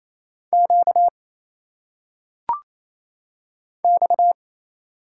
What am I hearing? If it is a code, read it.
Morse code, heard as a keyed tone, about 27 wpm, 700 Hz. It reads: Q X
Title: Single Letters - Phonetic Pronunciation 28wpm